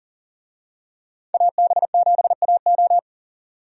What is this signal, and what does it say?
Morse code, heard as a keyed tone, about 38 wpm, 700 Hz. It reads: AB7AO
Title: USA Call Signs - 5 Characters - 40wpm